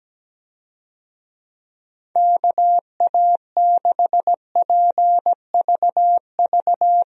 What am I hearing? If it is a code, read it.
Morse code, heard as a keyed tone, about 17 wpm, 700 Hz. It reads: KA6PVV
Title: USA Call Signs - 6 Characters - 17wpm